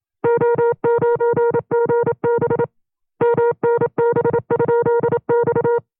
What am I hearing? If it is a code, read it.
Morse code, heard as a keyed tone, about 27 wpm, 470 Hz. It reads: O9GB MNB?X
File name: FreeFiles / CW-practice / 25WPM-side1